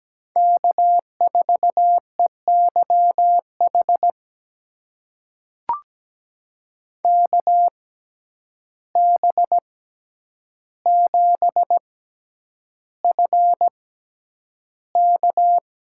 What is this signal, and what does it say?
Morse code, heard as a keyed tone, about 17 wpm, spaced as for 9 wpm, 700 Hz. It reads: K4EYH KB7FK
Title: USA Call Signs - 5 Characters - 6x Character Spacing - 17wpm